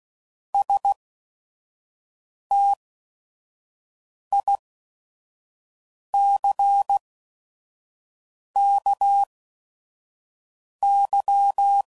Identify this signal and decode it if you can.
Morse code, heard as a keyed tone, about 16 wpm, spaced as for 5 wpm, 800 Hz. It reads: STICKY